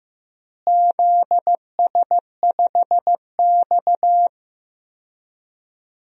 Morse code, heard as a keyed tone, about 15 wpm, 700 Hz. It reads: ZS5X
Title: DXCC-100 Call Signs - 4 Characters - Repeated 3x - 15wpm